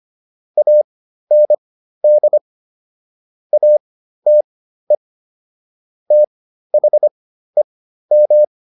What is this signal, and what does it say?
Morse code, heard as a keyed tone, about 25 wpm, spaced as for 13 wpm, 600 Hz. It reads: AND ATE THEM